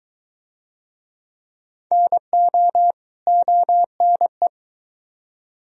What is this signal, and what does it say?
Morse code, heard as a keyed tone, about 22 wpm, 700 Hz. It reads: NO ONE